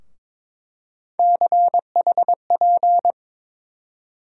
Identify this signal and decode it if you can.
Morse code, heard as a keyed tone, about 22 wpm, 700 Hz. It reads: CHP
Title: USA Call Sign Postfixes - 3 Letters 22wpm